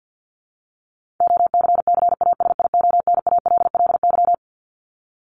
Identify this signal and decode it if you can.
Morse code, heard as a keyed tone, about 50 wpm, 700 Hz. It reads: OCCASIONALLY
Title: LY Suffix - Top 100 Words with Base Words 50wpm